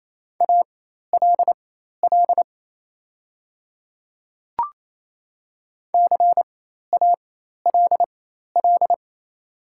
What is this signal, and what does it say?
Morse code, heard as a keyed tone, about 28 wpm, spaced as for 13 wpm, 700 Hz. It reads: ALL CALL
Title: 4x Character Spacing - Top 100 Words - 28wpm